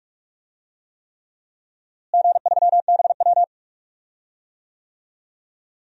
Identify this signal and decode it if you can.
Morse code, heard as a keyed tone, about 42 wpm, 700 Hz. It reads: G3BW
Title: DXCC-100 Call Signs - 4 Characters - Repeated 3x - 45wpm